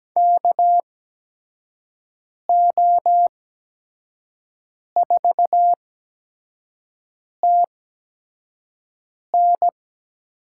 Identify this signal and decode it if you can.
Morse code, heard as a keyed tone, about 17 wpm, spaced as for 5 wpm, 700 Hz. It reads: KO4TN
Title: USA Call Signs - 5 Characters - 8x Character Spacing - 17wpm